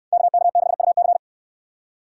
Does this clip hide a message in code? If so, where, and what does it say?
Yes: Morse code from 0.1 to 1.2 s, heard as a keyed tone, about 69 wpm, 700 Hz. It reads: KK6UC